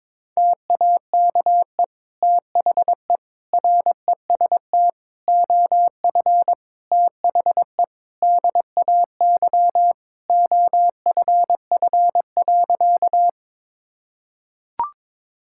Morse code, heard as a keyed tone, about 22 wpm, 700 Hz. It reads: TAKE THE REST OF THE DAY OFF.
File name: practice / 22wpm / Sentences with 2-Word Phrases - 22wpm